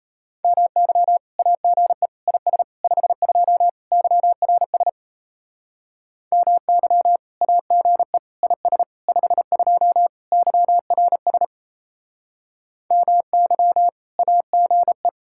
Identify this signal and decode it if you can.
Morse code, heard as a keyed tone, about 37 wpm, 700 Hz. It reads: MY AGE IS 52 YRS MY AGE IS 52 YRS MY AGE